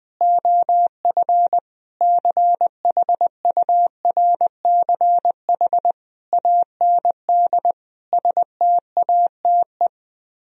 Morse code, heard as a keyed tone, about 20 wpm, 700 Hz. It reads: OF CHURCH AND STATE
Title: Speed-Racing - Sets of 5 Words 20wpm